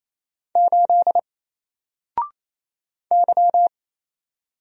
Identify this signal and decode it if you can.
Morse code, heard as a keyed tone, about 28 wpm, 700 Hz. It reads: 8 Y
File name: Single-Letter-Number-28wpm